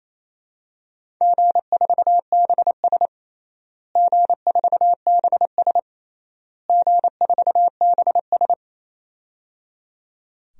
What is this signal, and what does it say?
Morse code, heard as a keyed tone, about 28 wpm, 700 Hz. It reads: G4BS G4BS G4BS
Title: DXCC-100 Call Signs - 4 Characters - Repeated 3x - 28wpm